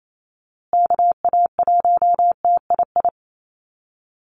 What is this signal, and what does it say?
Morse code, heard as a keyed tone, about 28 wpm, 700 Hz. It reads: KA1TII
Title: USA Call Signs Repeated 3x 28wpm